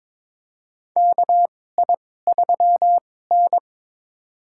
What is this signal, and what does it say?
Morse code, heard as a keyed tone, about 22 wpm, spaced as for 16 wpm, 700 Hz. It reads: KI3N
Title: USA Call Signs - 4 Characters - 2x Character Spacing - 22wpm